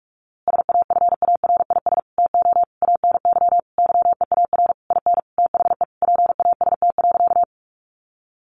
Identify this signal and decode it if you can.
Morse code, heard as a keyed tone, about 44 wpm, 700 Hz. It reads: SAFARIS TO ANY YEAR IN THE PAST.
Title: A Sound Of Thunder 45wpm